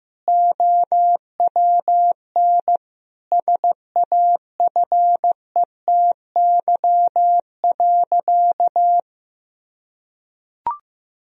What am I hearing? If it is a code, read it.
Morse code, heard as a keyed tone, about 15 wpm, 700 Hz. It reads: OWN SAFETY.
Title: Sentences from Top 1000 Words - Review Part 4 - 15wpm